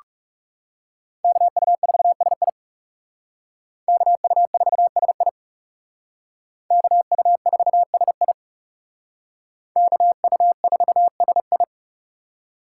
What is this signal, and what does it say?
Morse code, heard as a keyed tone, about 39 wpm, 700 Hz. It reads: KU4SI KU4SI KU4SI KU4SI